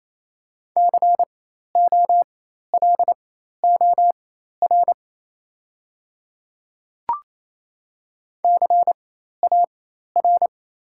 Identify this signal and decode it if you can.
Morse code, heard as a keyed tone, about 28 wpm, spaced as for 13 wpm, 690 Hz. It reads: COLOR CAR
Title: 4x Character Spacing - Top 500 Words - 28wpm